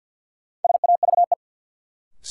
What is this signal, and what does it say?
Morse code, heard as a keyed tone, about 48 wpm, 700 Hz. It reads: SAVE
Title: Sentences from Top 1100 Words - New Word Drill Alt - 50wpm